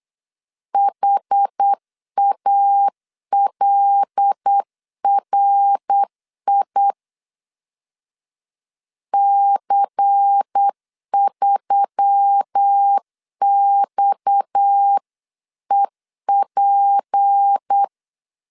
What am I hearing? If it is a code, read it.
Morse code, heard as a keyed tone, about 8 wpm, 800 Hz. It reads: HALRI C3XEP
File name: 8wpm